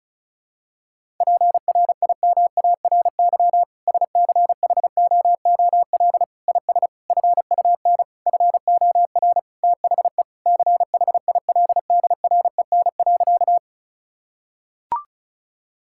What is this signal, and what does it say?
Morse code, heard as a keyed tone, about 36 wpm, 700 Hz. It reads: PRIMARY SCHOOL IS FUN FOR THE CHILDREN.
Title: Sentences from Top 1000 Words - Review Part 5 - 35wpm